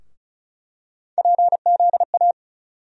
Morse code, heard as a keyed tone, about 36 wpm, 700 Hz. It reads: PZA
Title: USA Call Sign Postfixes - 3 Letters 35wpm